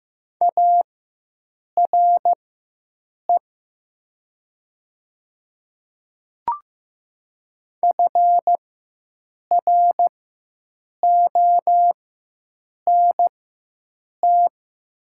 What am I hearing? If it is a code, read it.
Morse code, heard as a keyed tone, about 15 wpm, spaced as for 7 wpm, 700 Hz. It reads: ARE FRONT
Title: 4x Character Spacing - Top 500 Words - 15wpm